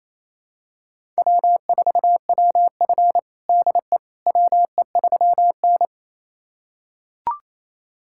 Morse code, heard as a keyed tone, about 28 wpm, 700 Hz. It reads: W4WF DE WE3N
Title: USA Callsign DE Callsign - 4 Characters - 28wpm